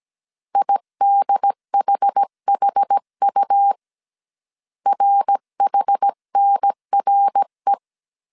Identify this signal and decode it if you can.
Morse code, heard as a keyed tone, about 17 wpm, 800 Hz. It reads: IDHHU RHNRE
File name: CW / DET / 1 3 1500C S A 16wpm 2141